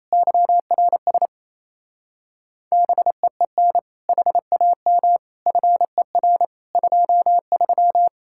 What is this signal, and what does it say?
Morse code, heard as a keyed tone, about 37 wpm, 700 Hz. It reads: YRS BEEN HAM FER 23